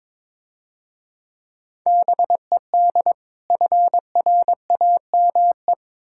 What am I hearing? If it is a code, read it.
Morse code, heard as a keyed tone, about 26 wpm, 690 Hz. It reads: BED FRAME
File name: Easy Words - Speed-Racing with Hints - 22wpm